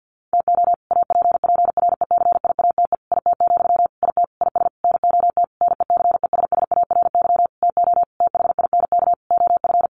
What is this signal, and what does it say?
Morse code, heard as a keyed tone, about 50 wpm, 700 Hz. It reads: TO APPRECIATE IT, IT IS NOT NECESSARY TO THINK OF
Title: Sentences from Top 1300 Words 50wpm